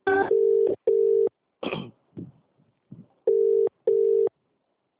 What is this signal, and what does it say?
Ringback tone (UK), 400 + 450 Hz.